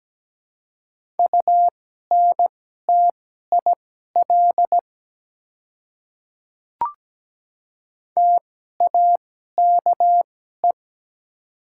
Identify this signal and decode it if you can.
Morse code, heard as a keyed tone, about 17 wpm, spaced as for 12 wpm, 700 Hz. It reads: UNTIL TAKE
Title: 2x Character Spacing - Top 300 Words - 17wpm